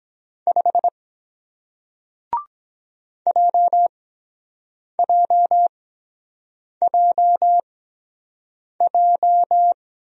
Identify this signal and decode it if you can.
Morse code, heard as a keyed tone, about 23 wpm, spaced as for 7 wpm, 700 Hz. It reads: 5 JJJJ